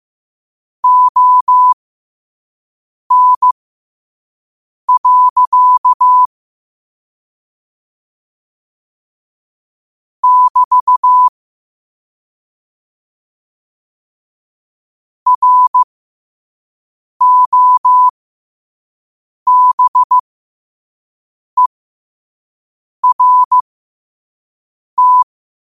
Morse code, heard as a keyed tone, about 15 wpm, spaced as for 5 wpm, 1000 Hz. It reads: ON. <BT> ROBERT
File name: qotdfiles / 5wpm